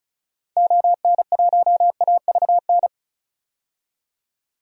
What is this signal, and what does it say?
Morse code, heard as a keyed tone, about 36 wpm, 700 Hz. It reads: ON1AVN
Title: DXCC-100 Call Signs - 6 Characters - 35wpm